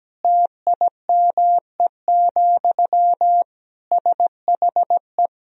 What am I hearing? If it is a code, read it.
Morse code, heard as a keyed tone, about 17 wpm, 700 Hz. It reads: TIME, SHE
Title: Sentences from Top 1000 Words - Review Part 2 - 17wpm